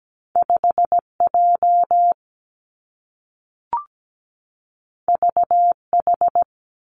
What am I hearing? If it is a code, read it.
Morse code, heard as a keyed tone, about 17 wpm, 700 Hz. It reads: 5 J V H
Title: Two Challenging Letter-Number Combinations 17wpm